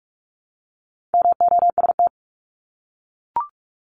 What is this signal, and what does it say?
Morse code, heard as a keyed tone, about 44 wpm, 700 Hz. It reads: MOST
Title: Top 200 Words 45wpm